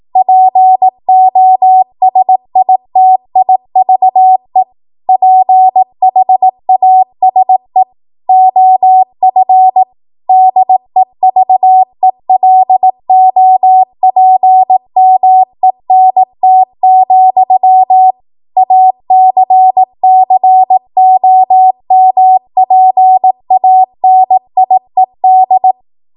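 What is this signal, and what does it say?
Morse code, heard as a keyed tone, about 18 wpm, 750 Hz. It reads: POSITIVE PHASE OF DEVELOPMENT, ACCOMPANIED
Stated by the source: ARRL 18 wpm Morse code practice archive transmission.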